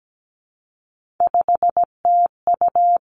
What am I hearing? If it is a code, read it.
Morse code, heard as a keyed tone, about 17 wpm, 700 Hz. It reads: 5TU
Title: Three Challenging Letter-Number Combinations 17wpm